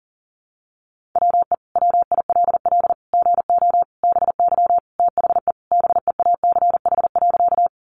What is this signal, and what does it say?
Morse code, heard as a keyed tone, about 40 wpm, 700 Hz. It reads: WE WILL GO BY THE BEACH.
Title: Sentences from Top 1500 Words 40wpm